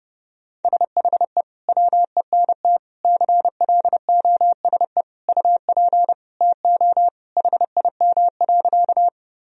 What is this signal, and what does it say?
Morse code, heard as a keyed tone, about 32 wpm, 700 Hz. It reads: SHE WENT CLOSE UP TO HIM.
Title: Sentences from Top 1000 Words - Review Part 1 - 30wpm